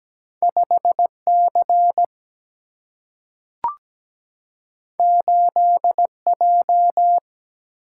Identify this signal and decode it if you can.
Morse code, heard as a keyed tone, about 17 wpm, 700 Hz. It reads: 5C 8J
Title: Single-Letter Single-Number 17wpm